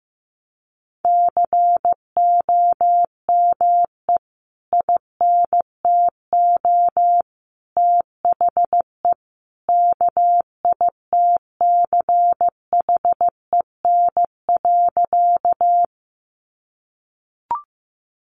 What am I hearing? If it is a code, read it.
Morse code, heard as a keyed tone, about 15 wpm, 700 Hz. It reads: COME INTO THE KITCHEN.
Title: Sentences from Top 1400 Words 15wpm